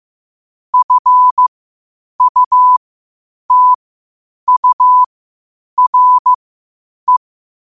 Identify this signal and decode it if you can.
Morse code, heard as a keyed tone, about 15 wpm, spaced as for 8 wpm, 1000 Hz. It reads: FUTURE
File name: qotdfiles / 8wpm